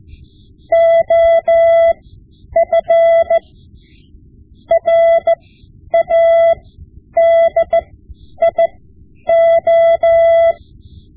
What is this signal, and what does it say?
Morse code, heard as a keyed tone, about 15 wpm, spaced as for 9 wpm, 670 Hz. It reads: OF RADIO
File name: Morse Lessons / Morse Lessons / Tape-5B